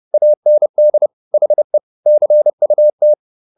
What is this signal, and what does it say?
Morse code, heard as a keyed tone, about 30 wpm, 600 Hz. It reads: AND HE CUT
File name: the old man and the sea / the old man and the sea 1 30wpm